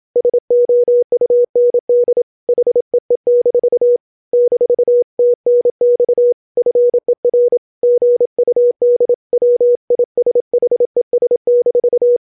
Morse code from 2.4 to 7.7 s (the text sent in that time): HEE- -TNX FER